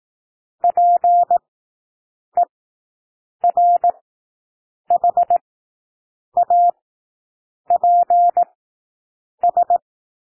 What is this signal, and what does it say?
Morse code, heard as a keyed tone, about 18 wpm, spaced as for 7 wpm, 700 Hz. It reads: PERHAPS
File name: SOUND WORDS 18 wpm